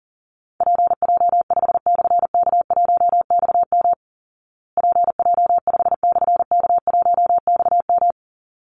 Morse code, heard as a keyed tone, about 40 wpm, 700 Hz. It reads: PJ5/K1XM PJ5/K1XM
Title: Call Signs Repeated 3x - 40wpm